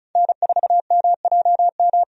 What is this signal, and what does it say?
Morse code, heard as a keyed tone, about 34 wpm, 700 Hz. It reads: N4MJM
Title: USA Call Signs - 5 Characters - 35wpm